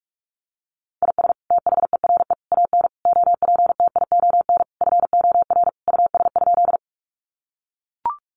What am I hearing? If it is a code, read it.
Morse code, heard as a keyed tone, about 44 wpm, 700 Hz. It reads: IS THERE AN OPTION FOR US?
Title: Sentences from Top 1300 Words 45wpm